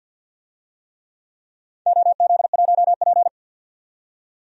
Morse code, heard as a keyed tone, about 50 wpm, 700 Hz. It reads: OZ1P
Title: DXCC-100 Call Signs - 4 Characters - 50wpm